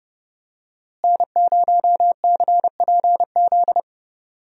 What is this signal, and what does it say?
Morse code, heard as a keyed tone, about 31 wpm, 700 Hz. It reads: N0CPZ
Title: USA Call Signs - 5 Characters - Repeated 3x - 30wpm